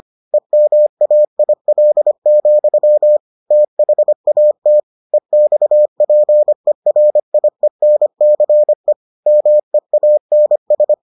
Morse code, heard as a keyed tone, about 25 wpm, 600 Hz. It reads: EMAIL, THAT EXPERIENCE MEANS